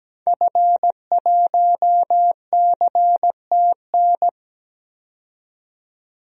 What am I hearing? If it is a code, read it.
Morse code, heard as a keyed tone, about 17 wpm, 700 Hz. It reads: F1CTN
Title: DXCC-100 Call Signs - 5 Characters - 17wpm